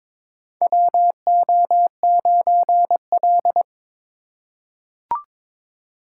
Morse code, heard as a keyed tone, about 22 wpm, 700 Hz. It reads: WO9L
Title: USA Call Signs - 4 Characters - 4x Character Spacing - 22wpm